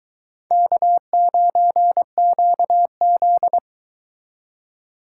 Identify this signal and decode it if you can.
Morse code, heard as a keyed tone, about 22 wpm, 700 Hz. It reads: K9QZ